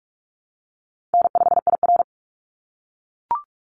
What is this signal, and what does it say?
Morse code, heard as a keyed tone, about 44 wpm, 700 Hz. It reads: N5IR
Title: Easy Call Signs - 45wpm